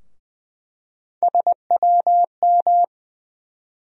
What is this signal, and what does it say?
Morse code, heard as a keyed tone, about 21 wpm, 700 Hz. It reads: SWM
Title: USA Call Sign Postfixes - 3 Letters 20wpm